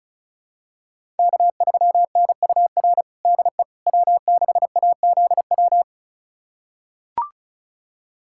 Morse code, heard as a keyed tone, about 36 wpm, 700 Hz. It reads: K3NUR DE W6AZW
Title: USA Callsign DE Callsign - 5 Characters - 35wpm